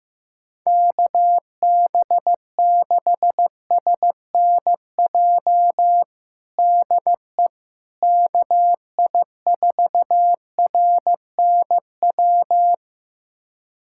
Morse code, heard as a keyed tone, about 15 wpm, 700 Hz. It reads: KB6SNJ DE KI4RNW
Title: USA Callsign DE Callsign - 6 Characters - 15wpm